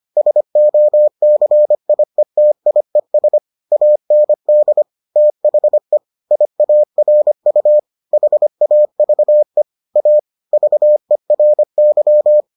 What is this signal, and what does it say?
Morse code, heard as a keyed tone, about 25 wpm, 600 Hz. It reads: SOCIETIES AND THE IARU HAVE A VERY